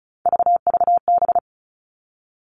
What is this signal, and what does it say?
Morse code, heard as a keyed tone, about 36 wpm, 700 Hz. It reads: VVB
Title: Three Challenging Letter-Number Combinations 35wpm